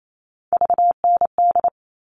Morse code, heard as a keyed tone, about 28 wpm, 700 Hz. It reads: VND
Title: All Cut Numbers - 3-Digit 28wpm